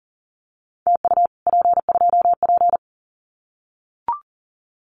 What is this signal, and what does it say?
Morse code, heard as a keyed tone, about 40 wpm, 700 Hz. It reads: TU P2P
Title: Common QSO Phrases Encore 40wpm